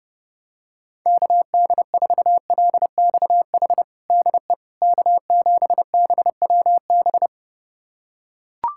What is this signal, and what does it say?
Morse code, heard as a keyed tone, about 32 wpm, 700 Hz. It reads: KD4LXH DE K7BWB
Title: USA Callsign DE Callsign 30wpm